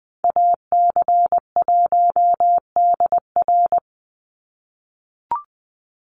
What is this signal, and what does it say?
Morse code, heard as a keyed tone, about 20 wpm, 700 Hz. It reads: AC1DR
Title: USA Call Signs 20wpm